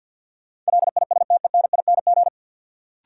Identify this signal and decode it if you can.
Morse code, heard as a keyed tone, about 54 wpm, 700 Hz. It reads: LISTENING
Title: Sets of 5 Words - New Word Drill - 50wpm